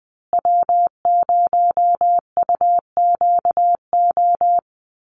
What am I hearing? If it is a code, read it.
Morse code, heard as a keyed tone, about 20 wpm, 700 Hz. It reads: W0UQO
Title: USA Call Signs Repeated 2x 20wpm